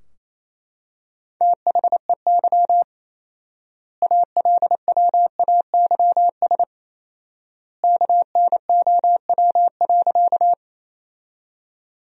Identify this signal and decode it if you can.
Morse code, heard as a keyed tone, about 28 wpm, 700 Hz. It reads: THEY ALWAYS KNOW.
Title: 4x Word-Spacing - Sentences from Top 300 Words - 28wpm